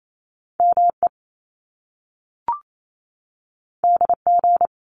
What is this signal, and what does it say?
Morse code, heard as a keyed tone, about 28 wpm, 700 Hz. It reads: ME DG
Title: Top Two Letter Combinations in Words 28wpm